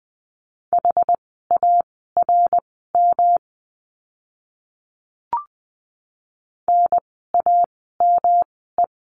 Morse code, heard as a keyed tone, about 20 wpm, spaced as for 15 wpm, 700 Hz. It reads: HARM NAME
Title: 108 Lesson 18f - Letter m - Words 20wpm